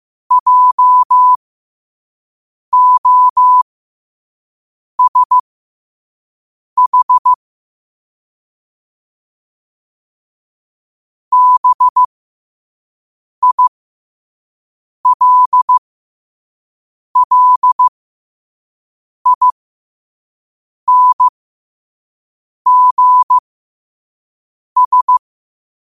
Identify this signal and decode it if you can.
Morse code, heard as a keyed tone, about 15 wpm, spaced as for 5 wpm, 1000 Hz. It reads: JOSH BILLINGS
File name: qotdfiles / 5wpm